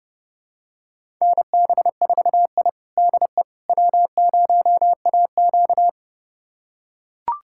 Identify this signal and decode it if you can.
Morse code, heard as a keyed tone, about 32 wpm, 700 Hz. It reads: NB4I DE W0AQ
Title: USA Callsign DE Callsign - 4 Characters - 30wpm